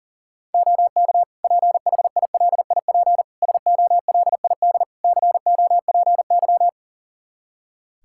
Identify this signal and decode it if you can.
Morse code, heard as a keyed tone, about 40 wpm, 700 Hz. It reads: OK PHILIP SOLID COPY
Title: K1USN SST QSO Practice 40wpm